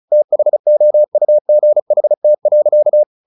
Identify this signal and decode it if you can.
Morse code, heard as a keyed tone, about 34 wpm, 600 Hz. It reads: THOUGHT.
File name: the old man and the sea / the old man and the sea 1 35wpm